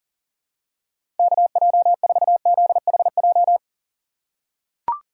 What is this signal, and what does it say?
Morse code, heard as a keyed tone, about 40 wpm, 700 Hz. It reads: KJ4ZHJ
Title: USA Call Signs - 6 Characters - 4x Character Spacing - 40wpm